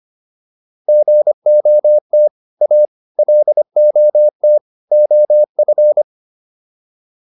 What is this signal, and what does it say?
Morse code, heard as a keyed tone, about 25 wpm, 600 Hz. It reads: GOT A LOT OF